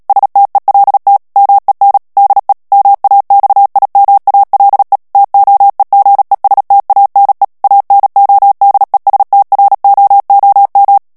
Morse code, heard as a keyed tone, about 37 wpm, 800 Hz. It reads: STELT MEN DE MAXIMALE TOEGESTANE ANODESTROOM
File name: morse mp3 / HOLLANDAIS texte clair 2 vitesse 35 mots minute